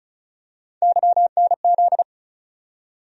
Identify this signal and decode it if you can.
Morse code, heard as a keyed tone, about 36 wpm, 700 Hz. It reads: YNZ
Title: USA Call Sign Postfixes - 3 Letters 35wpm